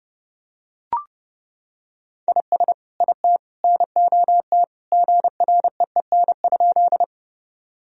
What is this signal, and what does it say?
Morse code, heard as a keyed tone, about 30 wpm, 700 Hz. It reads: IS IT NOT GREEN?
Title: Sentences from Top 1000 Words - Review Part 2 - 30wpm